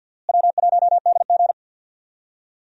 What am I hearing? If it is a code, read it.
Morse code, heard as a keyed tone, about 52 wpm, 700 Hz. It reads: W1DG
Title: USA Call Signs - 4 Characters - Repeated 3x - 50wpm